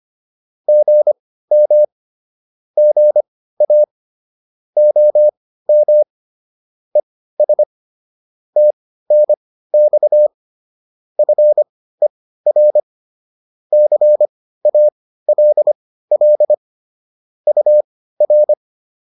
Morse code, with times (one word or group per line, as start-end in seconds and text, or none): 0.7-1.8 GM
2.8-3.8 GA
4.8-6.0 OM
7.0-7.6 ES
8.6-10.3 TNX
11.2-12.8 FER
13.7-16.5 CALL
17.5-18.5 UR